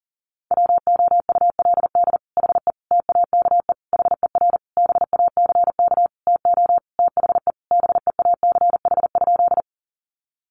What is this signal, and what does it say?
Morse code, heard as a keyed tone, about 40 wpm, 700 Hz. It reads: WOULD HE TAKE HER BACK TO THE BEACH?
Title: Sentences from Top 1500 Words 40wpm